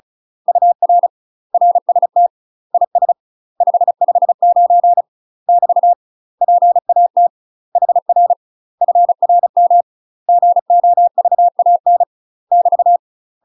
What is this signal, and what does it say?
Morse code, heard as a keyed tone, about 34 wpm, 700 Hz. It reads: UR RST IS 559 <BT> PAT HR FRM GOVAN <BT>
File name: UK MASTER QSOs TRAINING / 35WPM / UK MASTER QSOs 0037